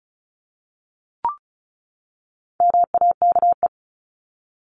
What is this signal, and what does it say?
Morse code, heard as a keyed tone, about 34 wpm, 700 Hz. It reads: MAKE
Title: Top 500 Words 35wpm